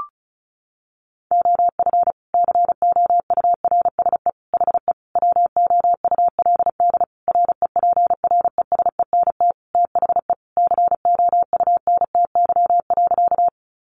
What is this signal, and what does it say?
Morse code, heard as a keyed tone, about 34 wpm, 700 Hz. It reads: OF COURSE HE WOULD REPRESENT THE COUNTY.
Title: Sentences from Top 1200 Words 35wpm